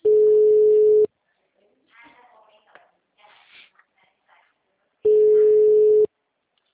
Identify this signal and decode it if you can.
Ringback tone (Europe, 425 Hz), with speech.